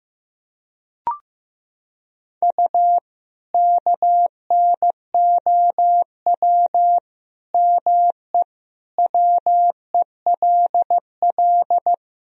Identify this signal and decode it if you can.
Morse code, heard as a keyed tone, about 16 wpm, 700 Hz. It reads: U KNOW ME WELL
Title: Sentences from Top 400 Words and Abbreviations 15wpm